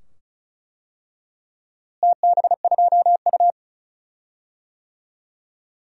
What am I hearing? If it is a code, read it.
Morse code, heard as a keyed tone, about 34 wpm, 690 Hz. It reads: TB2U
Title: DXCC-100 Call Signs - 4 Characters - Repeated 2x - 35wpm